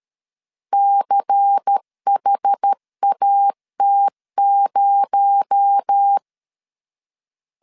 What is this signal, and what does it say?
Morse code, heard as a keyed tone, about 13 wpm, 800 Hz. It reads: CHAT0
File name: CW / DET / 1 6 1500C S B 12wpm 3431